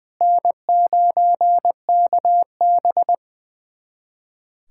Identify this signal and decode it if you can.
Morse code, heard as a keyed tone, about 20 wpm, 700 Hz. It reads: N9KB
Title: USA Callsign DE Callsign 20wpm